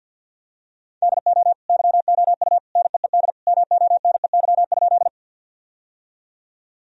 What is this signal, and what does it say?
Morse code, heard as a keyed tone, about 49 wpm, 700 Hz. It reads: DO YOU NEED MONEY?
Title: Sentences from Top 1000 Words - Review Part 1 - 50wpm